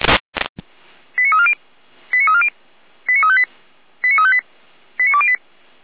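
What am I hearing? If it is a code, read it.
Five-tone selective call (ZVEI), heard as quick runs of five tones, tones of about 70 ms. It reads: 89340 89340 89358 89358 89208
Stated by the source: ZVEI five-tone selective call received on an SDR.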